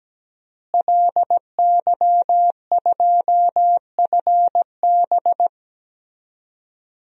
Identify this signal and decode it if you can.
Morse code, heard as a keyed tone, about 17 wpm, 700 Hz. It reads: LY2FB
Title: DXCC-100 Call Signs - 5 Characters - Repeated 2x - 17wpm